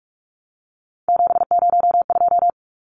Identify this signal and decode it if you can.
Morse code, heard as a keyed tone, about 44 wpm, 700 Hz. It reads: 702
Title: Numbers - 3-Digit 45wpm